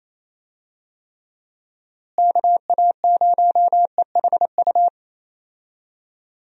Morse code, heard as a keyed tone, about 27 wpm, 700 Hz. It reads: KA0EHU
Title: USA Call Signs - 6 Characters - Repeated 3x - 28wpm